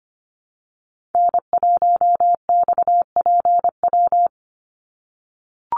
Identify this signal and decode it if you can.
Morse code, heard as a keyed tone, about 25 wpm, 700 Hz. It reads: N1XPW
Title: USA Call Signs 25wpm